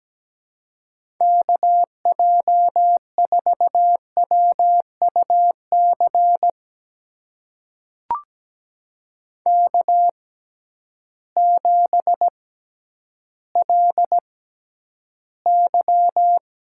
Morse code, heard as a keyed tone, about 17 wpm, spaced as for 10 wpm, 700 Hz. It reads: KJ4WUC K7LY
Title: USA Call Signs - 6x Character Spacing - 17wpm